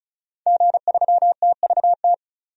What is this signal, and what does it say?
Morse code, heard as a keyed tone, about 36 wpm, 700 Hz. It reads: G3TVT
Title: DXCC-100 Call Signs - 5 Characters - 35wpm